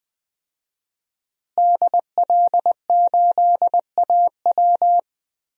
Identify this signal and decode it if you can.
Morse code, heard as a keyed tone, about 20 wpm, 700 Hz. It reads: DL8AW
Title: DXCC-100 Call Signs - 5 Characters - Repeated 2x - 20wpm